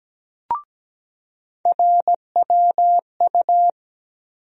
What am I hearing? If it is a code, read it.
Morse code, heard as a keyed tone, about 17 wpm, 700 Hz. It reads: RWU
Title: USA Call Sign Postfixes - 3 Letters 17wpm